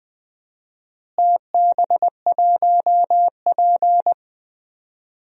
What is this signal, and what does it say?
Morse code, heard as a keyed tone, about 21 wpm, 700 Hz. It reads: TB1P
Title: DXCC-100 Call Signs - 4 Characters - Repeated 3x - 20wpm